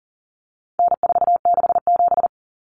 Morse code, heard as a keyed tone, about 40 wpm, 700 Hz. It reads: N467
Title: Common Cut Numbers - 4-Digit 40wpm